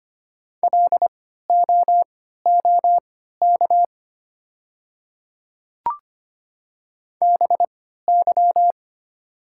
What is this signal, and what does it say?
Morse code, heard as a keyed tone, about 25 wpm, spaced as for 14 wpm, 700 Hz. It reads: LOOK BY
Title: 3x Character Spacing - Top 300 Words - 25wpm